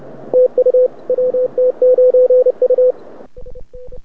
Morse code, heard as a keyed tone, about 28 wpm, 500 Hz. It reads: TU WT9U
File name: rec contest / arrl cw 2018 / test 32